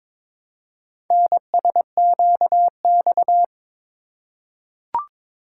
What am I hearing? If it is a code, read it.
Morse code, heard as a keyed tone, about 21 wpm, 700 Hz. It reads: NSQX
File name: practice / 22wpm / Four Letter-Number Combinations - 22wpm